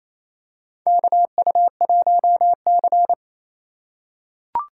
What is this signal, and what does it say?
Morse code, heard as a keyed tone, about 28 wpm, 700 Hz. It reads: KU1C
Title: USA Call Signs - 4 Characters - 2x Character Spacing - 28wpm